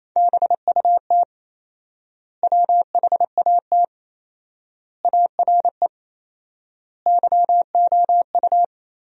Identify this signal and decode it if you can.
Morse code, heard as a keyed tone, about 28 wpm, 700 Hz. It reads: BUT WHAT ARE YOU
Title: 4x Word-Spacing - Sentences from Top 100 Words - 28wpm